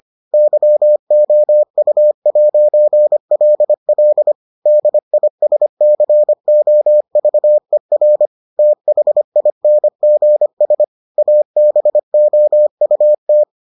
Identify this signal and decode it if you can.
Morse code, heard as a keyed tone, about 25 wpm, 600 Hz. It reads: YOU'LL DISCOVER THINGS ABOUT